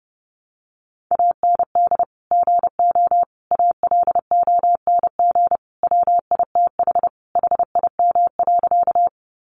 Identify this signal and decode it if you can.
Morse code, heard as a keyed tone, about 30 wpm, 700 Hz. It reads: AND GO ALONG WITH HIM.
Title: Sentences from Top 700 Words 30wpm